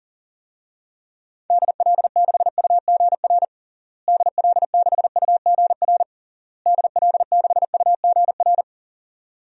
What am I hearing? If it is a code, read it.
Morse code, heard as a keyed tone, about 37 wpm, 700 Hz. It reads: DL6UGR DL6UGR DL6UGR
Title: DXCC-100 Call Signs - 6 Characters - Repeated 3x - 40wpm